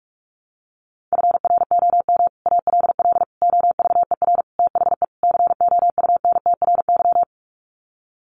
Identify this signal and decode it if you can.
Morse code, heard as a keyed tone, about 44 wpm, 700 Hz. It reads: FROM ALL OVER THE COUNTRY
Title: Sets of 5 Words Encore 45wpm